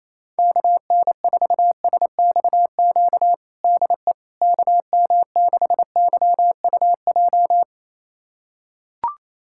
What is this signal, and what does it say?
Morse code, heard as a keyed tone, about 28 wpm, 700 Hz. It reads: KN4SXQ DE KM6YUJ
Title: USA Callsign DE Callsign 28wpm